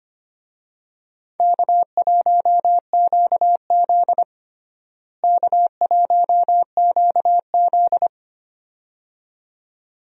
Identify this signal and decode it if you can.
Morse code, heard as a keyed tone, about 25 wpm, 700 Hz. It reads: K1QZ K1QZ
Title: USA Call Signs - 4 Characters - Repeated 2x - 25wpm